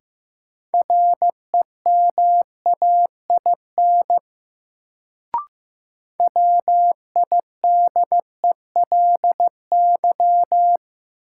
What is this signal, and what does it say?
Morse code, heard as a keyed tone, about 15 wpm, 700 Hz. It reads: REMAIN WIDELY
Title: Sentences from Top 1200 Words - New Word Drill Alt - 15wpm